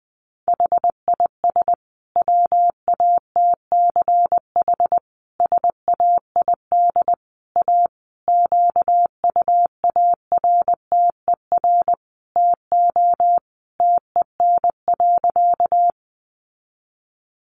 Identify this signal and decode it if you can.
Morse code, heard as a keyed tone, about 19 wpm, 700 Hz. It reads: HIS WATCH SAID A QUARTER TO TEN.
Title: Sentences from Top 1300 Words 20wpm